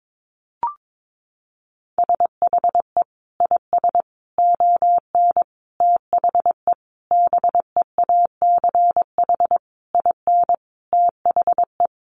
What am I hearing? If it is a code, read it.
Morse code, heard as a keyed tone, about 22 wpm, 700 Hz. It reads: SHE IS ON THE BEACH IN THE
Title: Sentences from Top 1500 Words 22wpm